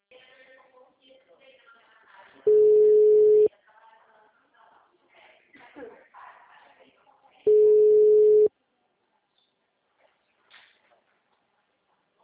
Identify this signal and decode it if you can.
Ringback tone (Europe, 425 Hz), with speech.